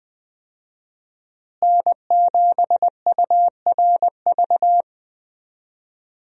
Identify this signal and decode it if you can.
Morse code, heard as a keyed tone, about 20 wpm, 700 Hz. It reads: N7URV
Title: USA Call Signs - 5 Characters - Repeated 3x - 20wpm